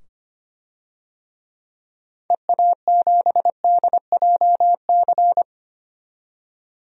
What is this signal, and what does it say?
Morse code, heard as a keyed tone, about 25 wpm, 700 Hz. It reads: EA7DJC
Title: DXCC-100 Call Signs - 6 Characters - 25wpm